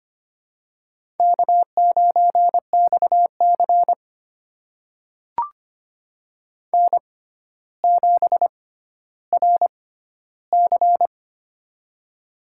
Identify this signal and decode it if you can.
Morse code, heard as a keyed tone, about 25 wpm, spaced as for 13 wpm, 700 Hz. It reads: K9XC N7RC
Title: USA Call Signs - 4 Characters - 6x Character Spacing - 25wpm